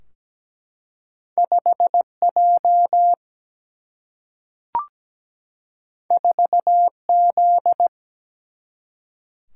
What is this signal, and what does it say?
Morse code, heard as a keyed tone, about 17 wpm, 700 Hz. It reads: 5J 4Z
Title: Single-Letter Single-Number 17wpm